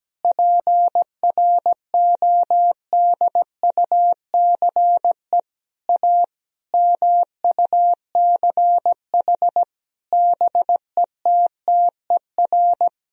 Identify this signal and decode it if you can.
Morse code, heard as a keyed tone, about 17 wpm, 700 Hz. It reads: PRODUCE A MUCH BETTER
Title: Sentences from Top 1000 Words - Review Part 3 - 17wpm